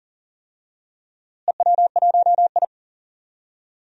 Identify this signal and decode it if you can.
Morse code, heard as a keyed tone, about 40 wpm, 700 Hz. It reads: EW1I
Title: DXCC-100 Call Signs - 4 Characters - Repeated 3x - 40wpm